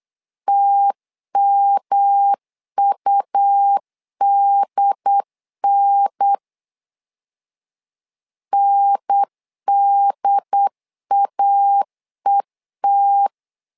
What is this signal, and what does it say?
Morse code, heard as a keyed tone, about 8 wpm, 800 Hz. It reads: TMUDN NDAET
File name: LEZIONI / 3 MWRO / 3 M 8wpm 2300